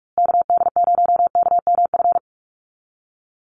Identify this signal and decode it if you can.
Morse code, heard as a keyed tone, about 44 wpm, 700 Hz. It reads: KD0KMF
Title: USA Call Signs Repeated 2x 45wpm